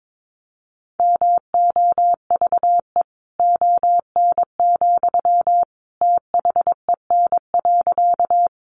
Morse code, heard as a keyed tone, about 22 wpm, 700 Hz. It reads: MOVE ON, THEN.
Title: Sentences from Top 200 Words 22wpm